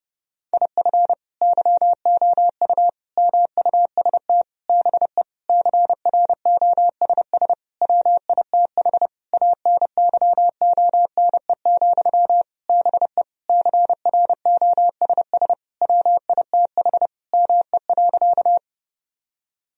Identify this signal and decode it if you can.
Morse code, heard as a keyed tone, about 32 wpm, 700 Hz. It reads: IF YOU MUST BE CROSS WITH ANYONE, BE CROSS WITH ME.
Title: Sentences from Top 1000 Words - Review Part 1 - 30wpm